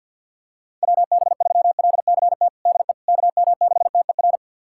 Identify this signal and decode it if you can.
Morse code, heard as a keyed tone, about 53 wpm, 700 Hz. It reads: WB3LCT DE KM6TER
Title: USA Callsign DE Callsign 50wpm